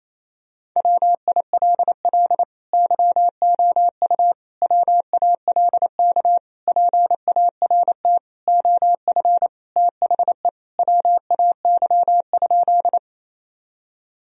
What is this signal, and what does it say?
Morse code, heard as a keyed tone, about 28 wpm, 700 Hz. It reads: WILL YOU WALK PART OF THE WAY?
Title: Sentences from Top 1000 Words - Review Part 1 - 28wpm